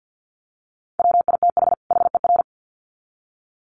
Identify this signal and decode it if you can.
Morse code, heard as a keyed tone, about 50 wpm, 700 Hz. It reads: WITH HER
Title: Sets of 2 Words 50wpm